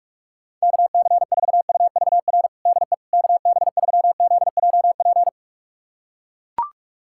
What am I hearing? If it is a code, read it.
Morse code, heard as a keyed tone, about 42 wpm, 700 Hz. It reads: KC4UVR DE KB3ZJP
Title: USA Callsign DE Callsign 45wpm